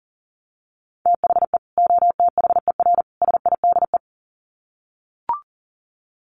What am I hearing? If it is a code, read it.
Morse code, heard as a keyed tone, about 40 wpm, 700 Hz. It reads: THE OTHER SIDE
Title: Sets of 3 Words 40wpm